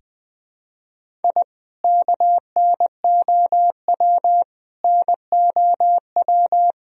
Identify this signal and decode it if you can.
Morse code, heard as a keyed tone, about 20 wpm, 700 Hz. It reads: I KNOW NOW
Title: Sentences from Top 1000 Words - Review Part 1 - 20wpm